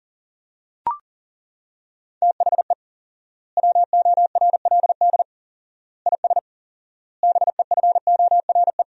Morse code, heard as a keyed tone, about 40 wpm, 700 Hz. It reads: THE WORLD IS BEFORE
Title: 4x Word-Spacing - Sentences from Top 200 Words - 40wpm